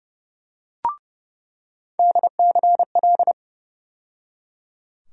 Morse code, heard as a keyed tone, about 30 wpm, 700 Hz. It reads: DCL
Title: USA Call Sign Postfixes - 3 Letters 30wpm